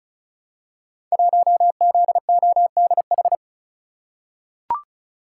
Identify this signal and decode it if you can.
Morse code, heard as a keyed tone, about 36 wpm, 700 Hz. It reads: 1ZODH
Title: Code Groups - 5 Letter:Numbers - 35wpm